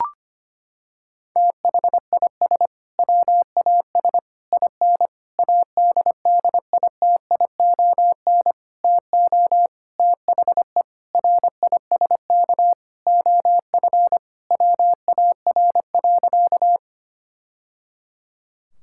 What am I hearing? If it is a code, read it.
Morse code, heard as a keyed tone, about 25 wpm, 700 Hz. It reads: THIS WAS IN ADDITION TO THE RISK OF WAR.
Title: Sentences from Top 1000 Words - Review Part 6 - 25wpm